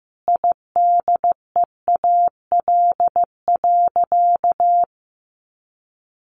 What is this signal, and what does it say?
Morse code, heard as a keyed tone, about 15 wpm, 700 Hz. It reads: IDEAL.
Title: Sentences from Top 1400 Words 15wpm